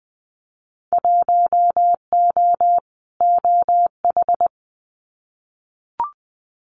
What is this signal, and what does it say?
Morse code, heard as a keyed tone, about 20 wpm, 700 Hz. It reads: 1O OH
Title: Winter Field Day Exchanges 20wpm